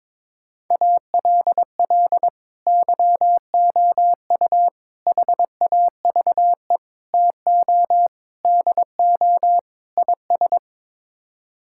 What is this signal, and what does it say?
Morse code, heard as a keyed tone, about 21 wpm, 700 Hz. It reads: ALL YOU HAVE TO DO IS
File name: practice / 22wpm / Sets of 6 Words - 22wpm